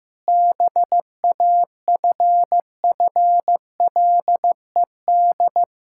Morse code, heard as a keyed tone, about 16 wpm, 700 Hz. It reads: BAFFLED
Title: Pangram Challenge v2 - 15wpm